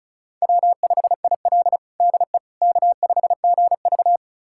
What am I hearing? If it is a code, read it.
Morse code, heard as a keyed tone, about 36 wpm, 700 Hz. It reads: W5IL DE K5GV
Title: USA Callsign DE Callsign - 4 Characters - 35wpm